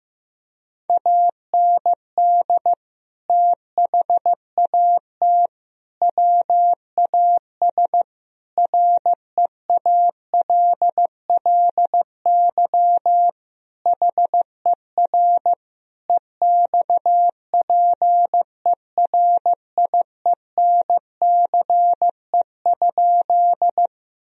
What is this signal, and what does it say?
Morse code, heard as a keyed tone, about 15 wpm, 700 Hz. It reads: AND THAT WAS REALLY HER EXPERIENCE?
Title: Sentences from Top 1000 Words - Review Part 3 - 15wpm